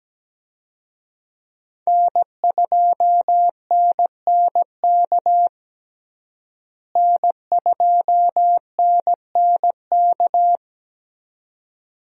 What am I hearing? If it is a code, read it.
Morse code, heard as a keyed tone, about 17 wpm, 700 Hz. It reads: N2NNK N2NNK
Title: USA Call Signs - 5 Characters - Repeated 2x - 17wpm